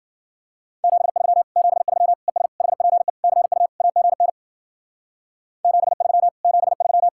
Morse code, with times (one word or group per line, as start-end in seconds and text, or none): 0.8-1.4 73
1.6-2.1 73
2.3-2.5 ES
2.6-3.1 HPE
3.2-3.7 CU
3.8-4.3 AGN
5.6-6.3 73
6.4-7.1 73